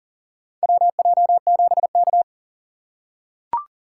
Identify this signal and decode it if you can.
Morse code, heard as a keyed tone, about 40 wpm, 700 Hz. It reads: WJ7K
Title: USA Call Signs - 4 Characters - 6x Character Spacing - 40wpm